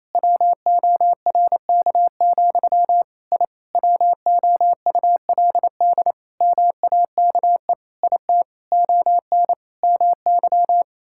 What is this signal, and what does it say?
Morse code, heard as a keyed tone, about 28 wpm, 700 Hz. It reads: WORK, I WOULD MAKE IT ON MY
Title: Sentences from Top 1000 Words - Review Part 6 - 28wpm